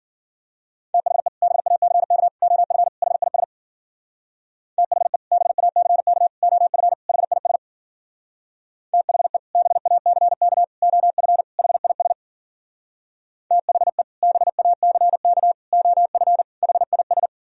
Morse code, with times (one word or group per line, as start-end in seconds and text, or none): 0.9-1.3 THE
1.4-2.3 BACK
2.4-2.9 OF
3.0-3.4 HIS
4.8-5.2 THE
5.3-6.3 BACK
6.4-6.9 OF
7.1-7.6 HIS
8.9-9.4 THE
9.6-10.6 BACK
10.8-11.4 OF
11.6-12.1 HIS
13.5-14.0 THE
14.2-15.5 BACK
15.7-16.4 OF
16.6-17.3 HIS